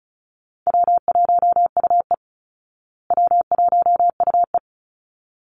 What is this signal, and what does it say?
Morse code, heard as a keyed tone, about 34 wpm, 700 Hz. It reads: W1UE W1UE
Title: Easy Call Signs Repeated 2x - 35wpm